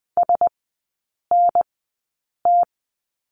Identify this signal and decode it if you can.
Morse code, heard as a keyed tone, about 20 wpm, spaced as for 8 wpm, 700 Hz. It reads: SNT
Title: 035 Lesson 7e - Letter s - Three Characters 20wpm